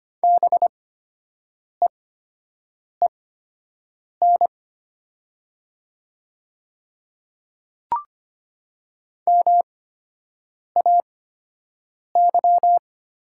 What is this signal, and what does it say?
Morse code, heard as a keyed tone, about 25 wpm, spaced as for 7 wpm, 700 Hz. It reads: BEEN MAY
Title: 8x Character Spacing - Top 200 Words - 25wpm